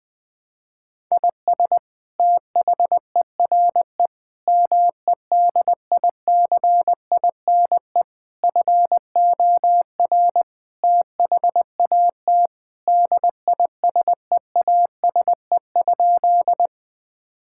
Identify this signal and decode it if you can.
Morse code, heard as a keyed tone, about 19 wpm, 700 Hz. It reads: IS THERE MEDICINE FOR THAT DISEASE?
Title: Sentences from Top 1000 Words - Review Part 6 - 20wpm